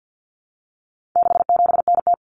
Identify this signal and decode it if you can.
Morse code, heard as a keyed tone, about 50 wpm, 700 Hz. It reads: 67NT
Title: Common Cut Numbers - 4-Digit 50wpm